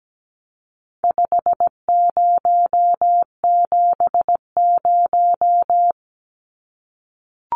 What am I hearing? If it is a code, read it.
Morse code, heard as a keyed tone, about 17 wpm, 700 Hz. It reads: 5070
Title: Numbers - 4-Digit 17wpm